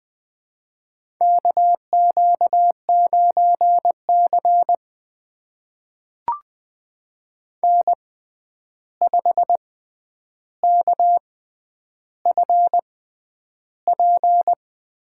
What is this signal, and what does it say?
Morse code, heard as a keyed tone, about 20 wpm, spaced as for 10 wpm, 700 Hz. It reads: KQ9C N5KFP
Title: USA Call Signs - 6x Character Spacing - 20wpm